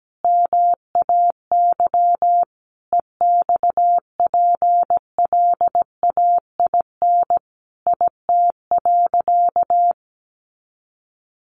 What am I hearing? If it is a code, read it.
Morse code, heard as a keyed tone, about 17 wpm, 700 Hz. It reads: MAY EXPLAIN IT.
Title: Sentences from Top 1100 Words 17wpm